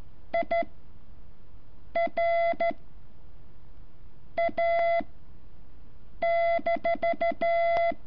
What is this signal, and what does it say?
Morse code, heard as a keyed tone, about 13 wpm, spaced as for 5 wpm, 680 Hz. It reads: IR A-
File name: curso / ejercicio 14